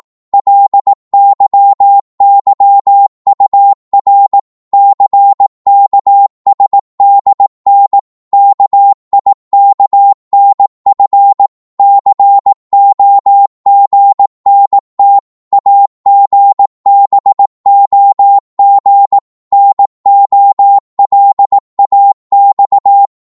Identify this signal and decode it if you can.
Morse code, heard as a keyed tone, about 18 wpm, 800 Hz. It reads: LYYUR CKSDN KIKNF COGNT AGBOG NOLAX
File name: Morse / ltr-18wpm